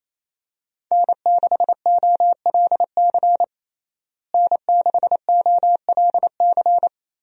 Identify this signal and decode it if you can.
Morse code, heard as a keyed tone, about 28 wpm, 700 Hz. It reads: N6OLC N6OLC
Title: USA Call Signs - 5 Characters - Repeated 2x - 28wpm